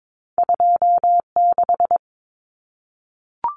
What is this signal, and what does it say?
Morse code, heard as a keyed tone, about 22 wpm, 700 Hz. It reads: 26
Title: Numbers - 2-Digit 22wpm